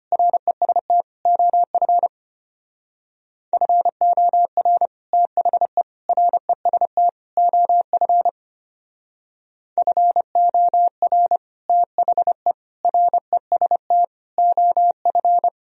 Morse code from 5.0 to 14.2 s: THE REST OF FOR THE REST

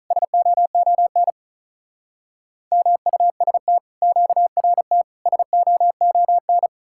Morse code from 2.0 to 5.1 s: MUST QRT

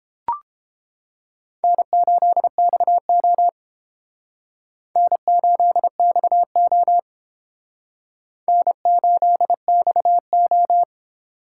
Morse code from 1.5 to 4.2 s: N8XO

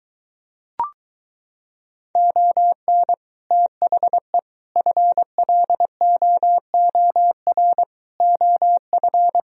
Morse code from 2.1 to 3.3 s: ON

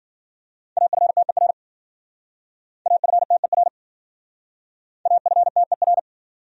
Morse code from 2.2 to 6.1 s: AFTER AFTER

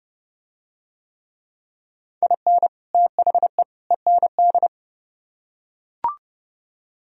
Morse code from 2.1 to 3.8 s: IN THE

